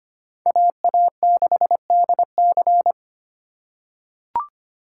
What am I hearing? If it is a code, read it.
Morse code, heard as a keyed tone, about 25 wpm, 700 Hz. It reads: AA6DC
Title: USA Call Signs - 5 Characters - 2x Character Spacing - 25wpm